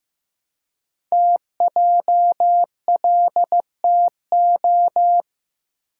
Morse code, heard as a keyed tone, about 15 wpm, 700 Hz. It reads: TJLTO
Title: Code Groups - 5 Letters - 15wpm